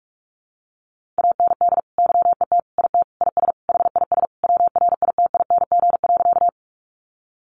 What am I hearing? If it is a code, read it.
Morse code, heard as a keyed tone, about 44 wpm, 700 Hz. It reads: AND YET IT IS HIS WRITING.
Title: Sentences from Top 800 Words 45wpm